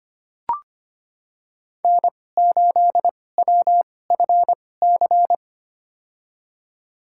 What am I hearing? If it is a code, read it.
Morse code, heard as a keyed tone, about 25 wpm, spaced as for 18 wpm, 700 Hz. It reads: N8WFC
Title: USA Call Signs - 5 Characters - 2x Character Spacing - 25wpm